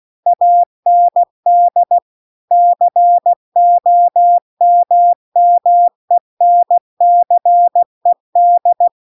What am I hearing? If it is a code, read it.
Morse code, heard as a keyed tone, about 16 wpm, 700 Hz. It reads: AND COMMENCED
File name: 16WPM